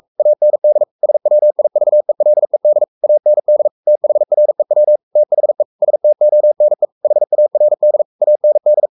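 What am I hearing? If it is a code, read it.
Morse code, heard as a keyed tone, about 44 wpm, 600 Hz. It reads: AND SWIVELED AND THREW THE STONE HARD AND
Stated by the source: Morse code at 43 wpm sending Ken Follett's novel Never in English, from CW med Gnister.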